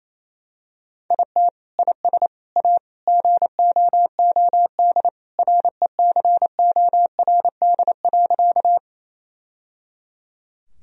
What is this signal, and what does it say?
Morse code, heard as a keyed tone, about 28 wpm, 700 Hz. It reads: IT IS A GOOD RECORD.
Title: Sentences from Top 1000 Words - Review Part 3 - 28wpm